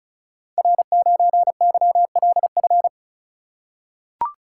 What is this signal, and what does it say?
Morse code, heard as a keyed tone, about 36 wpm, 700 Hz. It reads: R9YLF
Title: Code Groups - 5 Letter:Numbers - 35wpm